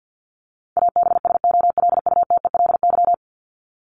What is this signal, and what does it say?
Morse code, heard as a keyed tone, about 50 wpm, 700 Hz. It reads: ABSOLUTELY
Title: LY Suffix - Top 100 Words with Base Words 50wpm